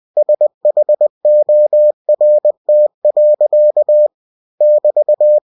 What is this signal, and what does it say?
Morse code, heard as a keyed tone, about 20 wpm, 600 Hz. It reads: SHORT. <BT>
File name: cw-books / winnie-the-pooh / winnie-20wpm-page-0074